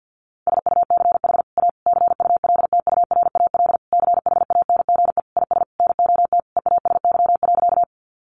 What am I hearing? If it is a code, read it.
Morse code, heard as a keyed tone, about 50 wpm, 700 Hz. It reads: SUCH A CULTURAL CHANGE IS NOT EASY.
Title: Sentences from Top 1500 Words 50wpm